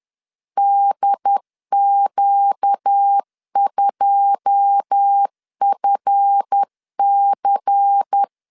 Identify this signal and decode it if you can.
Morse code, heard as a keyed tone, about 10 wpm, 800 Hz. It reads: DQ2FC
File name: LEZIONI / 9 2 10wpm 2736